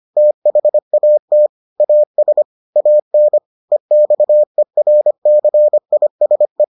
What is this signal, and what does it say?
Morse code, heard as a keyed tone, about 25 wpm, 600 Hz. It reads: THAT AS AN EXERCISE